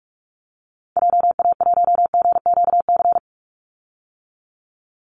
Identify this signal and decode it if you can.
Morse code, heard as a keyed tone, about 45 wpm, 700 Hz. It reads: JA1GQC
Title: Call Signs - 45wpm